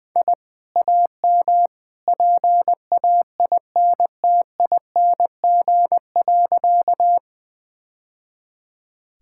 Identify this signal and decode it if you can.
Morse code, heard as a keyed tone, about 20 wpm, 700 Hz. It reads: I AM PAINTING.
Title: Sentences from Top 1000 Words - Review Part 6 - 20wpm